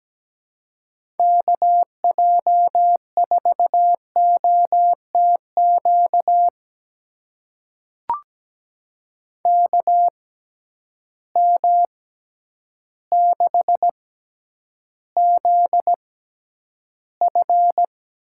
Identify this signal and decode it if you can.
Morse code, heard as a keyed tone, about 17 wpm, spaced as for 9 wpm, 700 Hz. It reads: KJ4OTQ KM6ZF
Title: USA Call Signs - 6 Characters - 6x Character Spacing - 17wpm